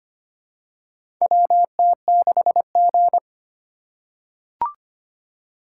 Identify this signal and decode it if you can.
Morse code, heard as a keyed tone, about 25 wpm, 690 Hz. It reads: WT6G
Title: USA Call Signs - 4 Characters - 4x Character Spacing - 25wpm